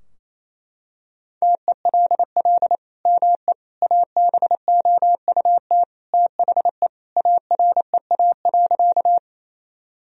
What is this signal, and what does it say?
Morse code, heard as a keyed tone, about 28 wpm, 700 Hz. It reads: TELL ME ABOUT THE AREA.
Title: Sentences from Top 1000 Words - Review Part 2 - 28wpm